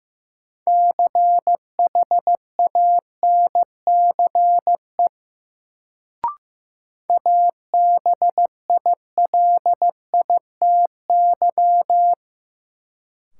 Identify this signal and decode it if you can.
Morse code, heard as a keyed tone, about 15 wpm, 700 Hz. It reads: CHANCE ABILITY
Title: Sets of 3 Words - New Word Drill - 15wpm